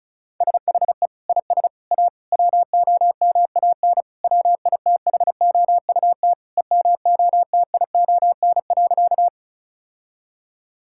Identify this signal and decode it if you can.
Morse code, heard as a keyed tone, about 34 wpm, 700 Hz. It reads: SHE IS A WOMAN WITHOUT EMOTION.
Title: Sentences from Top 1600 Words 35wpm